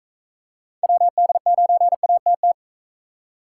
Morse code, heard as a keyed tone, about 40 wpm, 700 Hz. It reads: WD9ATT